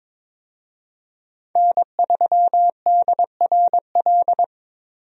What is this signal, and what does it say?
Morse code, heard as a keyed tone, about 22 wpm, 700 Hz. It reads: N3DRL
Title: USA Call Signs - 5 Characters - Repeated 2x - 22wpm